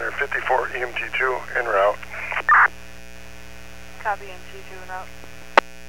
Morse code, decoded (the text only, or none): none